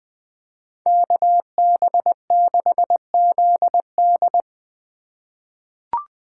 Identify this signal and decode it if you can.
Morse code, heard as a keyed tone, about 20 wpm, 700 Hz. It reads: KB6ZD
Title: USA Call Signs - 3x Character Spacing - 20wpm